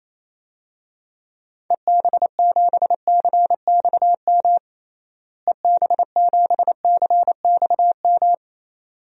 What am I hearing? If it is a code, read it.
Morse code, heard as a keyed tone, about 28 wpm, 700 Hz. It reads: EB7CXM EB7CXM
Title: DXCC-100 Call Signs - 6 Characters - Repeated 2x - 28wpm